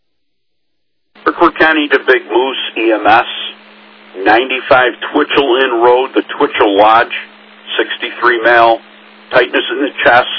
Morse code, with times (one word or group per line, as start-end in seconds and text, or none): none